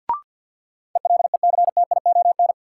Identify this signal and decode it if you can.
Morse code, heard as a keyed tone, about 52 wpm, 700 Hz. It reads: ELECTION
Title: Sentences from Top 1500 Words - New Word Drill Alt - 50wpm